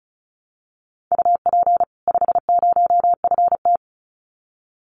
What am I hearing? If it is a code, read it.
Morse code, heard as a keyed tone, about 34 wpm, 700 Hz. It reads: UP 50FT
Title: Common QSO Phrases 35wpm